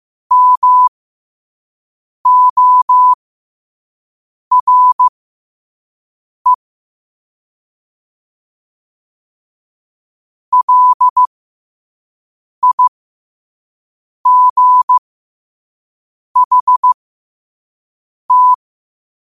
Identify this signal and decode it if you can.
Morse code, heard as a keyed tone, about 15 wpm, spaced as for 5 wpm, 1000 Hz. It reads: MORE LIGHT
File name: qotdfiles / 5wpm